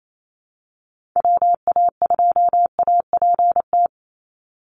Morse code, heard as a keyed tone, about 28 wpm, 700 Hz. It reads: WA2APT
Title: USA Call Signs 28wpm